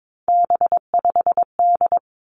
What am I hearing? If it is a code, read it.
Morse code, heard as a keyed tone, about 22 wpm, 700 Hz. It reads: B5D
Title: Three Challenging Letter-Number Combinations 22wpm